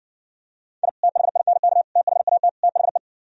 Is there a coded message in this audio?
Yes: Morse code from 0.8 to 3.0 s, heard as a keyed tone, about 69 wpm, 700 Hz. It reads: I THINK THAT THE